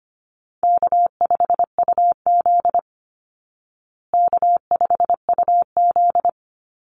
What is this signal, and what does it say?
Morse code, heard as a keyed tone, about 25 wpm, 700 Hz. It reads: K5UZ K5UZ
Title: Easy Call Signs Repeated 2x - 25wpm